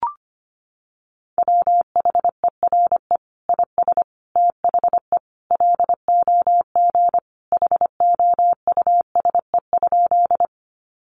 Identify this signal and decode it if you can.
Morse code, heard as a keyed tone, about 25 wpm, 700 Hz. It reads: WHERE IS THE LOG HOUSE?
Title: Sentences from Top 1400 Words 25wpm